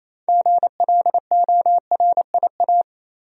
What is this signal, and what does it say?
Morse code, heard as a keyed tone, about 28 wpm, 700 Hz. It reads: GLORIA
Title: USA Common Names - 28wpm